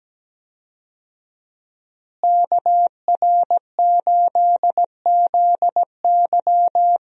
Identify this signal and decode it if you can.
Morse code, heard as a keyed tone, about 17 wpm, 700 Hz. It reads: KR8ZY
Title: USA Call Signs - 5 Characters - Repeated 3x - 17wpm